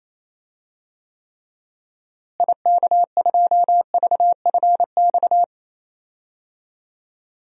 Morse code, heard as a keyed tone, about 28 wpm, 700 Hz. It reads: IK2VFX
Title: DXCC-100 Call Signs - 6 Characters - Repeated 2x - 28wpm